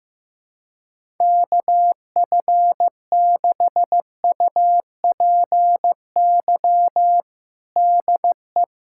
Morse code, heard as a keyed tone, about 15 wpm, 700 Hz. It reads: KF6UPY DE
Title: USA Callsign DE Callsign - 6 Characters - 15wpm